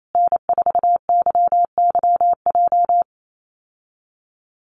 Morse code, heard as a keyed tone, about 28 wpm, 700 Hz. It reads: N4YYJ
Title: USA Call Signs Repeated 3x 28wpm